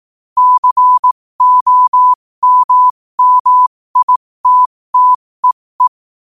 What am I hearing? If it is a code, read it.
Morse code, heard as a keyed tone, about 18 wpm, spaced as for 15 wpm, 1000 Hz. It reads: COMMITTEE
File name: qotdfiles / 15wpm